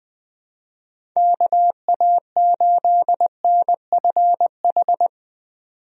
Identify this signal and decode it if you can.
Morse code, heard as a keyed tone, about 21 wpm, 700 Hz. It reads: KA8NFH
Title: USA Call Signs - 6 Characters - 20wpm